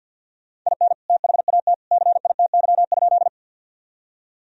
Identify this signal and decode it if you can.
Morse code, heard as a keyed tone, about 52 wpm, 700 Hz. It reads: IN THAT CITY?
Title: Sentences from Top 1000 Words - Review Part 1 - 50wpm